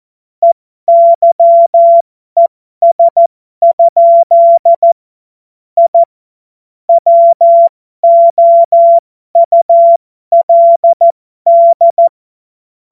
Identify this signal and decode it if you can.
Morse code, heard as a keyed tone, about 14 wpm, spaced as for 12 wpm, 680 Hz. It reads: EYES? I WOULD